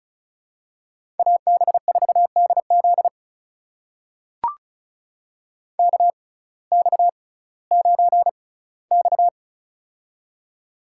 Morse code, heard as a keyed tone, about 36 wpm, spaced as for 19 wpm, 700 Hz. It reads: AB4DZ KX9X
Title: USA Call Signs - 6x Character Spacing - 35wpm